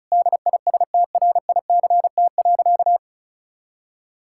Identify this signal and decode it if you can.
Morse code, heard as a keyed tone, about 36 wpm, 700 Hz. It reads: DISTRICT.
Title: Sentences from Top 1600 Words 35wpm